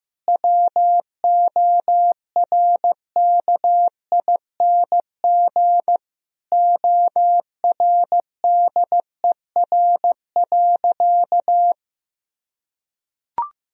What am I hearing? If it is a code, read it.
Morse code, heard as a keyed tone, about 15 wpm, 700 Hz. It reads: WORKING ORDER.
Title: Sentences from Top 1000 Words - Review Part 4 - 15wpm